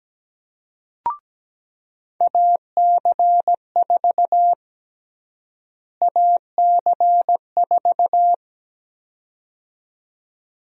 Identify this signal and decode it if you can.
Morse code, heard as a keyed tone, about 17 wpm, 700 Hz. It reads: AC4 AC4
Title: USA Call Sign Prefixes - Repeated 2x - 17wpm